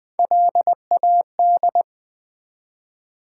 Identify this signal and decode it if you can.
Morse code, heard as a keyed tone, about 20 wpm, 700 Hz. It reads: LAD
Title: Build A Word - 1 Letter At A Time - 4-Letter Words - 20wpm